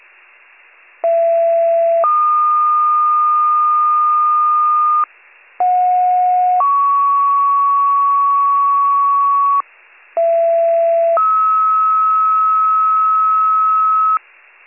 Two-tone page, heard as a one-second tone, then a three-second tone, about 670 then 1220 Hz, about 730 then 1120 Hz, about 670 then 1290 Hz. It is heard from 1.0 to 14.2 s.